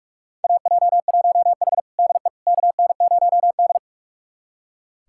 Morse code, heard as a keyed tone, about 46 wpm, 700 Hz. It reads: AJ1H DE KN0D
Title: USA Callsign DE Callsign - 4 Characters - 45wpm